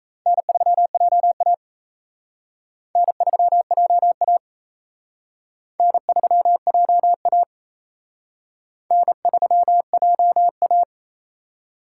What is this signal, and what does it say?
Morse code, heard as a keyed tone, about 34 wpm, 700 Hz. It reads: N3JA N3JA N3JA N3JA
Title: Speed Racing - USA 1x2 Call Signs 28wpm